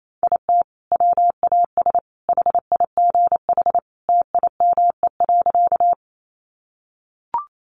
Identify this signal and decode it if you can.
Morse code, heard as a keyed tone, about 28 wpm, 700 Hz. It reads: IT WAS HIGH TIME.
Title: Sentences from Top 200 Words 28wpm